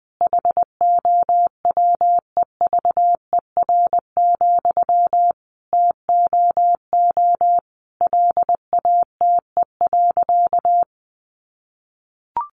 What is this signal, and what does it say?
Morse code, heard as a keyed tone, about 20 wpm, 700 Hz. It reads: HOWEVER, TOO LATE.
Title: Sentences from Top 600 Words 20wpm